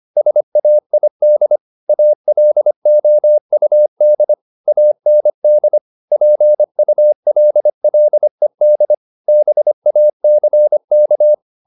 Morse code, heard as a keyed tone, about 25 wpm, 600 Hz. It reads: SAID ALOUD AND PULLED BACK
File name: the old man and the sea part 1 25wpm